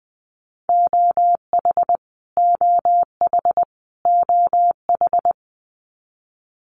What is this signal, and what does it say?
Morse code, heard as a keyed tone, about 20 wpm, 700 Hz. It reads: OH OH OH
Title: Top 500 Words Repeated 20wpm